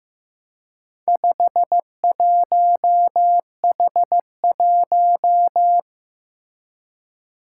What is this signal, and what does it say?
Morse code, heard as a keyed tone, about 15 wpm, 700 Hz. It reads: 51H1
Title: Four Challenging Letter-Number Combinations 15wpm